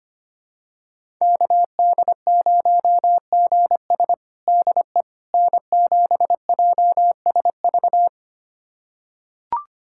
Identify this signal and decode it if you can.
Morse code, heard as a keyed tone, about 25 wpm, 700 Hz. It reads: KD0GS DE N7JSV
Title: USA Callsign DE Callsign - 5 Characters - 25wpm